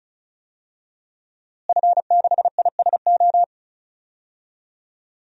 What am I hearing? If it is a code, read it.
Morse code, heard as a keyed tone, about 34 wpm, 700 Hz. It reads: F6ISO
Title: DXCC-100 Call Signs - 5 Characters - Repeated 3x - 35wpm